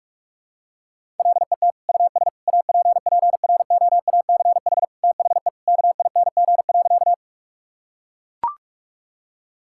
Morse code, heard as a keyed tone, about 41 wpm, 700 Hz. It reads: LET US APPROACH THE KING.
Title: Sentences from Top 1000 Words - Review Part 6 - 45wpm